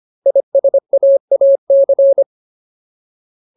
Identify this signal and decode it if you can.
Morse code, heard as a keyed tone, about 25 wpm, 550 Hz. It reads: ISAAC
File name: Ultimate Word List / 25WPM Ultimate